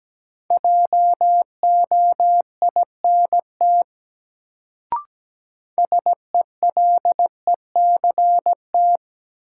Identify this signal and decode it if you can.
Morse code, heard as a keyed tone, about 17 wpm, 700 Hz. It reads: JOINT SELECT
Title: Sentences from Top 1300 Words - New Word Drill - 17wpm